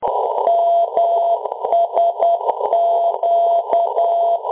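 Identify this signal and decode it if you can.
Morse code, heard as a keyed tone, about 10 wpm, 680 Hz. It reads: MSQ
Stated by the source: Low frequency aviation radio beacon; aviation LF/MF beacon.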